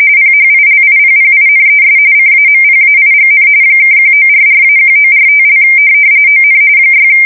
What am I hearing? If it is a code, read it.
RTTY, about 45.45 baud. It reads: PHILIPPINES CAMPAIGN (1944-1945) ON 6